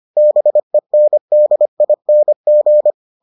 Morse code, heard as a keyed tone, about 25 wpm, 600 Hz. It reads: BENDING